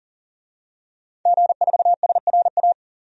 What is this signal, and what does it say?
Morse code, heard as a keyed tone, about 40 wpm, 700 Hz. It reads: G4SRA
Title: DXCC-100 Call Signs - 5 Characters - 40wpm